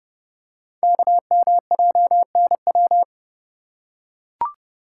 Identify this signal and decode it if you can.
Morse code, heard as a keyed tone, about 28 wpm, 700 Hz. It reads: KMJNW
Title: Code Groups - 5 Letters - 30wpm